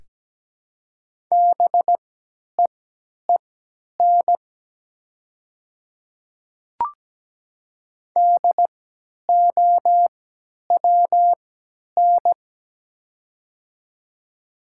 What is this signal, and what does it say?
Morse code, heard as a keyed tone, about 17 wpm, spaced as for 10 wpm, 700 Hz. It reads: BEEN DOWN
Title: 3x Character Spacing - Top 200 Words - 17wpm